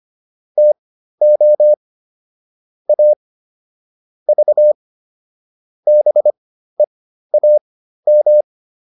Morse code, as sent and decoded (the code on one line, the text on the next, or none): - --- / .- / ...- / -... .- --
TO A V BEAM